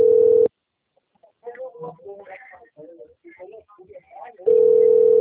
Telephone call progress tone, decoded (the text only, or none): ringback tone